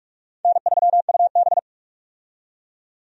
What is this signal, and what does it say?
Morse code, heard as a keyed tone, about 42 wpm, 700 Hz. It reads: N3UB
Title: USA Call Signs - 4 Characters - 45wpm